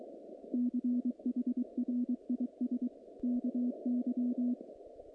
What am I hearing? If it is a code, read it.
Morse code, heard as a keyed tone, about 23 wpm, 250 Hz. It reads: CHRIS KY
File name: contests / naqp-cw-k5tr-op-k5pi- -2200